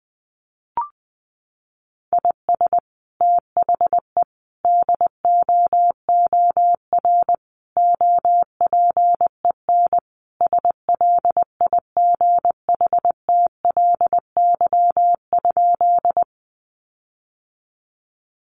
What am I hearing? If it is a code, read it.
Morse code, heard as a keyed tone, about 20 wpm, 700 Hz. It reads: IS THE DOOR OPEN SLIGHTLY?
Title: Sentences from Top 1300 Words 20wpm